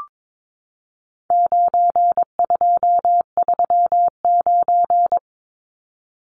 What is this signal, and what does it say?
Morse code, heard as a keyed tone, about 22 wpm, 700 Hz. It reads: 9239
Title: Numbers - 4-Digit 22wpm